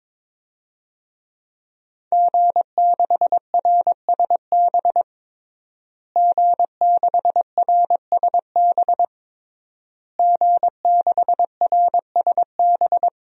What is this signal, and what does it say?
Morse code, heard as a keyed tone, about 22 wpm, 700 Hz. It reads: G6RSB G6RSB G6RSB
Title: DXCC-100 Call Signs - 5 Characters - Repeated 3x - 22wpm